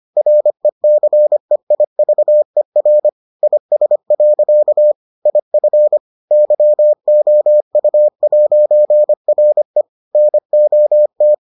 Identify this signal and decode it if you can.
Morse code, heard as a keyed tone, about 25 wpm, 600 Hz. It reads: RECEIVER IS. IF YOU'RE NOT